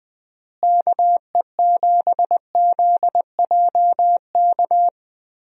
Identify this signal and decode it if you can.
Morse code, heard as a keyed tone, about 20 wpm, 700 Hz. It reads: KE7ZJK
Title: USA Call Signs - 6 Characters - Repeated 2x - 20wpm